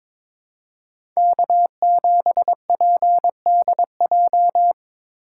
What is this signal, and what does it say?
Morse code, heard as a keyed tone, about 22 wpm, 700 Hz. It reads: K7PDJ
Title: USA Call Signs - 5 Characters - 6x Character Spacing - 22wpm